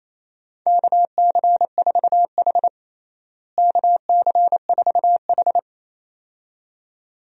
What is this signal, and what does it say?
Morse code, heard as a keyed tone, about 28 wpm, 700 Hz. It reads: KC4H KC4H
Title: USA Call Signs - 4 Characters - Repeated 2x - 28wpm